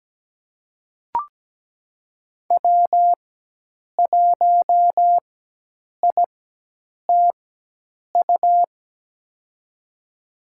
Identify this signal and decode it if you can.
Morse code, heard as a keyed tone, about 17 wpm, spaced as for 8 wpm, 700 Hz. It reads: W1ITU
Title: USA Call Signs - 5 Characters - 4x Character Spacing - 17wpm